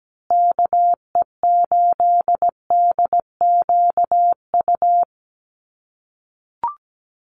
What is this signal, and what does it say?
Morse code, heard as a keyed tone, about 17 wpm, 700 Hz. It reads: KE8DQU
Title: USA Call Signs Repeated 3x 17wpm